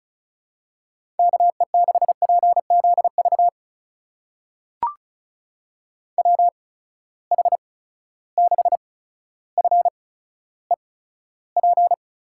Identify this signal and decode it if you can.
Morse code, heard as a keyed tone, about 36 wpm, spaced as for 15 wpm, 700 Hz. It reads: KE6PZV WH6FEP
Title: USA Call Signs - 6 Characters - 8x Character Spacing - 35wpm